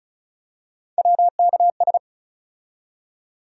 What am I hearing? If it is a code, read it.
Morse code, heard as a keyed tone, about 36 wpm, 700 Hz. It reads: WKS
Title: USA Call Sign Postfixes - 3 Letters 35wpm